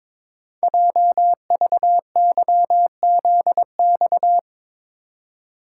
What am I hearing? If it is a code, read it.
Morse code, heard as a keyed tone, about 22 wpm, 700 Hz. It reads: JVYZX
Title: Code Groups - 5 Letters - 22wpm